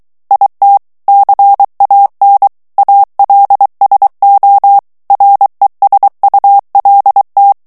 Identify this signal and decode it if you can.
Morse code, heard as a keyed tone, about 23 wpm, 800 Hz. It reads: IT CAN ALSO RESULT
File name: morse mp3 / ANGLAIS texte clair 1 vitesse 22 mots minute